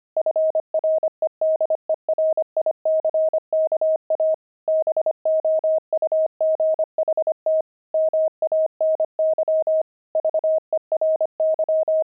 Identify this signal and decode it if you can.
Morse code, heard as a keyed tone, about 25 wpm, 630 Hz. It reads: FREDERICKA BOUGHT MANY VERY
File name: pangram-fs-25-cs-25-opal